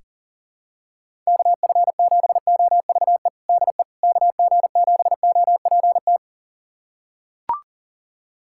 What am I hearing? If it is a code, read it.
Morse code, heard as a keyed tone, about 40 wpm, 690 Hz. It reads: KF7OVE DE KG7OPT
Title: USA Callsign DE Callsign - 6 Characters - 40wpm